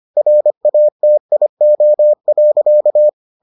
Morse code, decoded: RATIO.

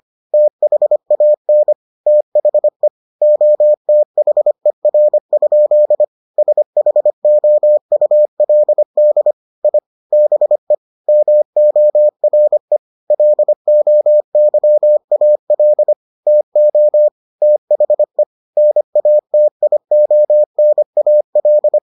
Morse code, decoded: THAN THE OTHER? SHOULD I BE MORE LOYAL TO THE NATIONAL

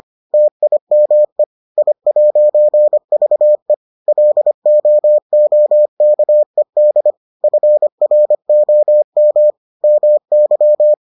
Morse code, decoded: TIME I'VE LOOKED FROM MY

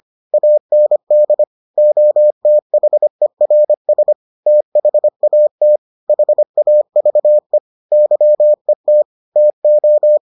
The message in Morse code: AND OTHERS THAT HAVE YET TO

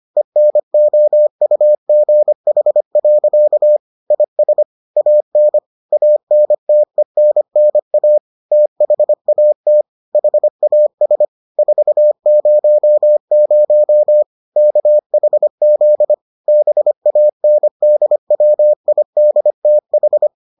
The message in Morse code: ENOUGH. IS AN ANTENNA THAT HAS 400 KHZ BANDWIDTH

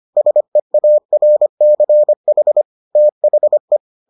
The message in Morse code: SEARCH THE

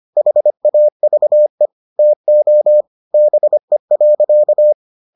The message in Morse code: HAVE TO BE.

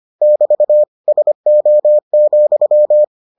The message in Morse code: <BT> SO,

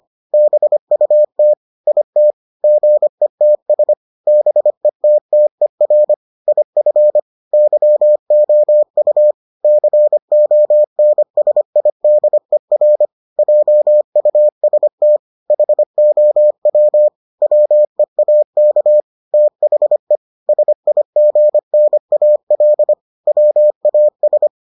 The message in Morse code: BUT IT GETS BETTER IF YOU CONSIDER JUST HOW WEAK THE SIGNAL WAS

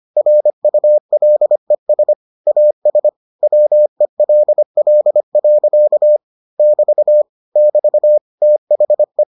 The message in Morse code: RULES AS WELL. <BT> <BT> THE

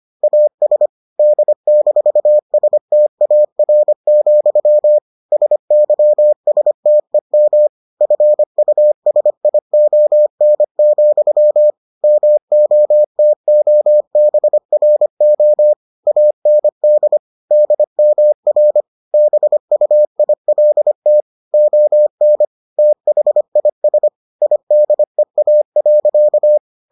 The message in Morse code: AS D-STAR, SYSTEM FUSION, MOTOBRO AND DMR BUILT ON THIS IDEA.

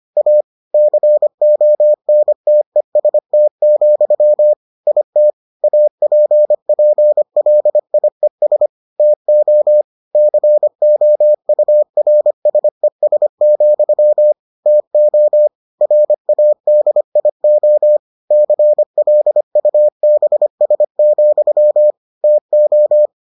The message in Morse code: A CONTEST, IT APPLIES TO COURSES, TO RADIO CLUBS, TO

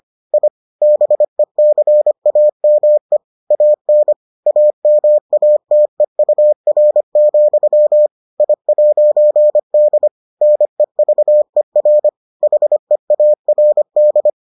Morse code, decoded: I BECAME AN AMATEUR, I'D NEVER HEARD

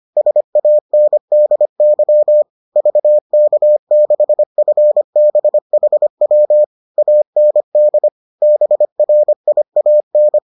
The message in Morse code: SANDY VK6FBHW AND BRIAN

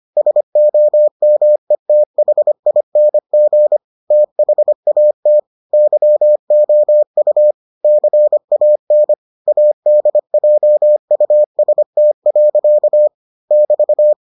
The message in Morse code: SOMETHING THAT YOU CAN ADJUST. <BT>

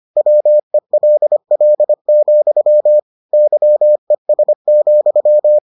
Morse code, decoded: WELL, YES,

